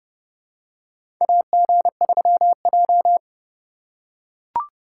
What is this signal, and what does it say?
Morse code, heard as a keyed tone, about 31 wpm, 700 Hz. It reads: AG3J
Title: USA Call Signs - 4 Characters - 2x Character Spacing - 30wpm